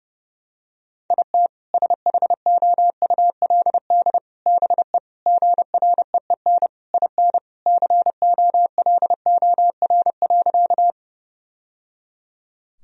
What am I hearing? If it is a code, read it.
Morse code, heard as a keyed tone, about 30 wpm, 700 Hz. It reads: IT SHOULD BE GREEN IN COLOR.
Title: Sentences from Top 1000 Words - Review Part 2 - 30wpm